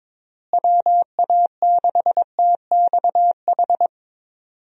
Morse code, heard as a keyed tone, about 22 wpm, 700 Hz. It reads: WA6TXH
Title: USA Call Signs - 6 Characters - Repeated 2x - 22wpm